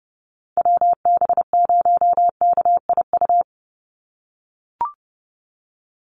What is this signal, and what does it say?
Morse code, heard as a keyed tone, about 30 wpm, 700 Hz. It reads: WB0KIU
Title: NA SOTA Chaser Callsigns 30wpm